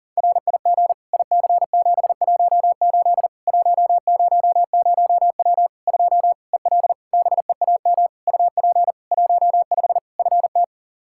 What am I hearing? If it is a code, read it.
Morse code, heard as a keyed tone, about 40 wpm, 700 Hz. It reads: RIG IC718 100W 2 EL BEAM UP 15 FT
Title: Practice QSO - Exchange 2 - Info - 40wpm